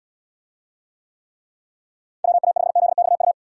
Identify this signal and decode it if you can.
Morse code, heard as a keyed tone, about 74 wpm, 700 Hz. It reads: WA5ZXU